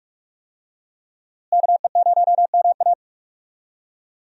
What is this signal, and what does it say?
Morse code, heard as a keyed tone, about 46 wpm, 700 Hz. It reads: KE0MA
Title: USA Call Signs - 5 Characters - Repeated 3x - 45wpm